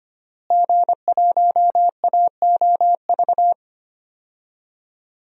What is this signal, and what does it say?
Morse code, heard as a keyed tone, about 25 wpm, 700 Hz. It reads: G1AOV
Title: DXCC-100 Call Signs - 5 Characters - Repeated 3x - 25wpm